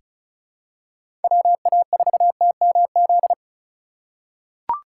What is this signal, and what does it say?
Morse code, heard as a keyed tone, about 36 wpm, 700 Hz. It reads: WA4TMZ
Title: USA Call Signs - 6 Characters - 2x Character Spacing - 35wpm